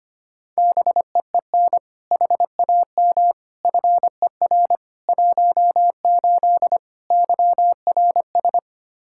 Morse code, heard as a keyed tone, about 25 wpm, 700 Hz. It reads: BEEN HAM FER 18 YRS
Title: Practice QSO - Exchange 3 - More Info - 25wpm